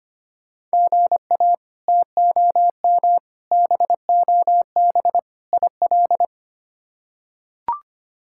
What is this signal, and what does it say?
Morse code, heard as a keyed tone, about 25 wpm, 700 Hz. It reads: GA TOM BOB IL
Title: K1USN SST QSO Practice - 25wpm